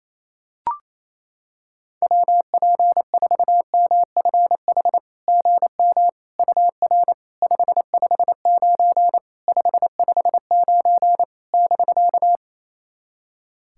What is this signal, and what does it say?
Morse code, heard as a keyed tone, about 27 wpm, 660 Hz. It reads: WP4MFH GM UR 559 559 <BK>
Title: SOTA QSO Practice 28wpm